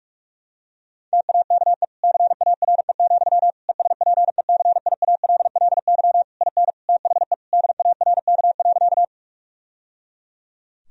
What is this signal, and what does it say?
Morse code, heard as a keyed tone, about 46 wpm, 700 Hz. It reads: TAKE CARE, ESPECIALLY IN THE DARK.
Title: Sentences from Top 1000 Words - Review Part 4 - 45wpm